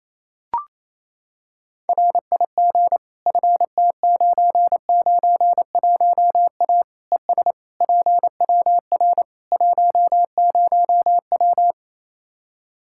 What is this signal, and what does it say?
Morse code, heard as a keyed tone, about 28 wpm, 700 Hz. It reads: RIG FT991A ES PWR 10W
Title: Practice QSO - Exchange 2 - Info - 28wpm